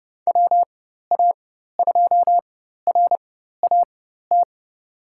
Morse code, heard as a keyed tone, about 29 wpm, spaced as for 14 wpm, 700 Hz. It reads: WA2RAT
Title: USA Call Signs - 4x Character Spacing - 30wpm